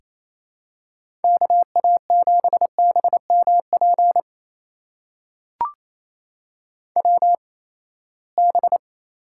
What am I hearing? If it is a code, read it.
Morse code, heard as a keyed tone, about 28 wpm, spaced as for 19 wpm, 700 Hz. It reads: KA7BMP WB
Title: USA Call Signs - 6 Characters - 8x Character Spacing - 28wpm